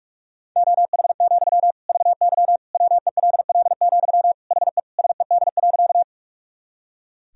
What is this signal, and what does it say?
Morse code, heard as a keyed tone, about 41 wpm, 700 Hz. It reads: OH, VY WELL, HE SED.
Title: Sentences from Top 500 Words and Abbreviations 45wpm